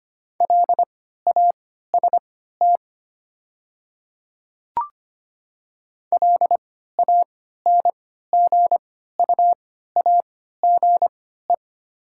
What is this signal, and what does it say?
Morse code, heard as a keyed tone, about 25 wpm, spaced as for 14 wpm, 690 Hz. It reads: LAST LANGUAGE
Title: 3x Character Spacing - Top 500 Words - 25wpm